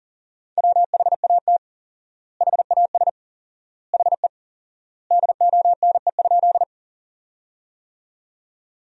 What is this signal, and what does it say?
Morse code, heard as a keyed tone, about 40 wpm, 700 Hz. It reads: WHAT HAS HE DONE?
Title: 4x Word-Spacing - Sentences from Top 500 Words - 40wpm